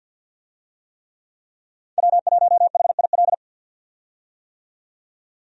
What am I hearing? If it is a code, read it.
Morse code, heard as a keyed tone, about 46 wpm, 700 Hz. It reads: W1HIL
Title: USA Call Signs - 5 Characters - 50wpm